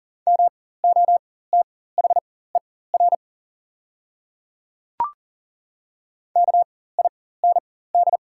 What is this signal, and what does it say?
Morse code, heard as a keyed tone, about 40 wpm, spaced as for 19 wpm, 700 Hz. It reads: MOTHER KIND
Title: 4x Character Spacing - Top 200 Words - 40wpm